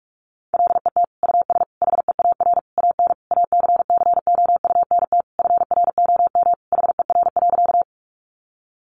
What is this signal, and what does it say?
Morse code, heard as a keyed tone, about 44 wpm, 700 Hz. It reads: LET US HEAR AN ACCOUNT FROM HER.
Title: Sentences from Top 900 Words 45wpm